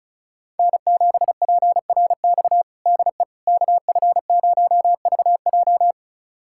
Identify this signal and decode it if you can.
Morse code, heard as a keyed tone, about 36 wpm, 700 Hz. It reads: N7PRX DE KF0VJ
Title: USA Callsign DE Callsign - 5 Characters - 35wpm